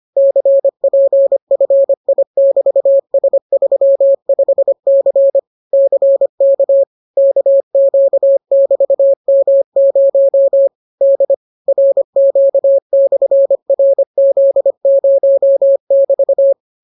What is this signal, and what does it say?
Morse code, heard as a keyed tone, about 25 wpm, 550 Hz. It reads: CPFI<BT>S35C CK KQ<BT>M0 D RQ/RZ0<BT>